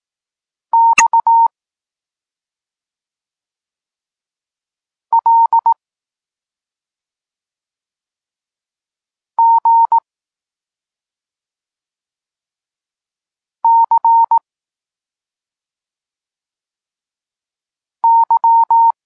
Morse code, heard as a keyed tone, about 18 wpm, 920 Hz. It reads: X L G C Y
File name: Tekst-34 4